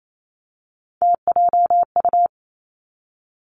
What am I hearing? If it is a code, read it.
Morse code, heard as a keyed tone, about 28 wpm, 690 Hz. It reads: TJU
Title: Three Challenging Letter-Number Combinations 28wpm